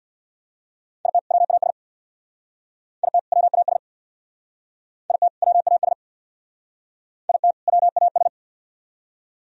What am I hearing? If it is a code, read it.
Morse code, heard as a keyed tone, about 64 wpm, 700 Hz. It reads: IT WAS IT WAS IT WAS IT WAS